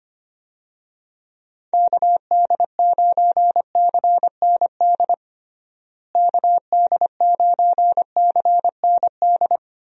Morse code, heard as a keyed tone, about 25 wpm, 700 Hz. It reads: KD9CND KD9CND
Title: USA Call Signs - 6 Characters - Repeated 2x - 25wpm